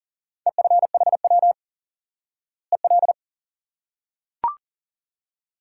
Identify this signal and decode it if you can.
Morse code, heard as a keyed tone, about 40 wpm, 700 Hz. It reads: EFHW EL
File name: IQE Course 3 - Lesson 89 - YAGI - Two Elements - 40wpm